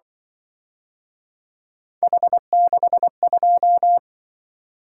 Morse code, heard as a keyed tone, about 24 wpm, 700 Hz. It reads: H62
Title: Speed-Racing - Three Letter-Number Combinations - 20wpm